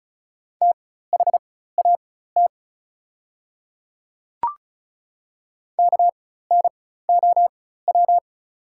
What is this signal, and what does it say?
Morse code, heard as a keyed tone, about 34 wpm, spaced as for 16 wpm, 690 Hz. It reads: THAT KNOW
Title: 4x Character Spacing - Top 500 Words - 35wpm